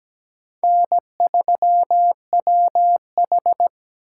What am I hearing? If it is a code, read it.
Morse code, heard as a keyed tone, about 20 wpm, 700 Hz. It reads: N3WH
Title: Speed Racing - USA 1x2 Call Signs 17wpm